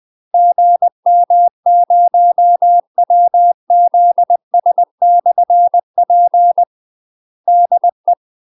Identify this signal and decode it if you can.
Morse code, heard as a keyed tone, about 20 wpm, 700 Hz. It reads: GM0WZS/P DE
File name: UK MASTER QSOs TRAINING / 20WPM / UK MASTER QSOs 0008